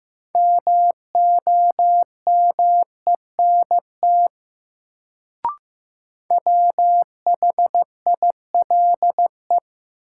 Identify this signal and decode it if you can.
Morse code, heard as a keyed tone, about 15 wpm, 700 Hz. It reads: MOMENT WHILE
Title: Sets of 3 Words - New Word Drill - 15wpm